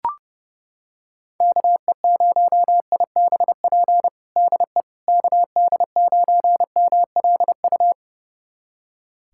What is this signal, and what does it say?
Morse code, heard as a keyed tone, about 30 wpm, 700 Hz. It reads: KE0IBP DE KD9MLU
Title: USA Callsign DE Callsign - 6 Characters - 30wpm